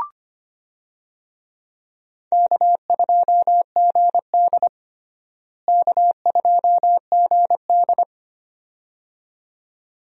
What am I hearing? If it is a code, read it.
Morse code, heard as a keyed tone, about 25 wpm, 700 Hz. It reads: K2GD K2GD
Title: USA Call Signs - 4 Characters - Repeated 2x - 25wpm